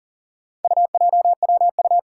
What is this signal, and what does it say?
Morse code, heard as a keyed tone, about 40 wpm, 700 Hz. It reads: UJWU
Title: Four Challenging Letter-Number Combinations 40wpm